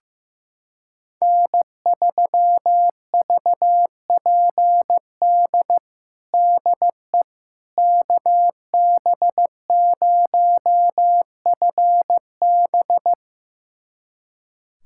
Morse code, heard as a keyed tone, about 15 wpm, 700 Hz. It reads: N3VPD DE KB0FB
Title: USA Callsign DE Callsign - 5 Characters - 15wpm